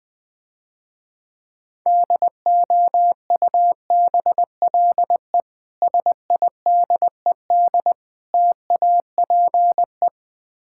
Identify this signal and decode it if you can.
Morse code, heard as a keyed tone, about 20 wpm, 700 Hz. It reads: DOUBLE SIDED TAPE
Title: Easy Words - Speed-Racing with Hints - 20wpm